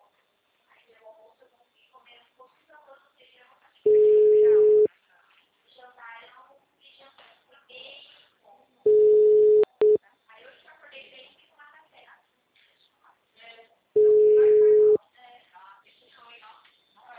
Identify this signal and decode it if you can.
Ringback tone (Europe, 425 Hz), with speech.